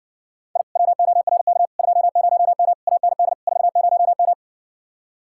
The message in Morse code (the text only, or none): I WORK 20M AND 40M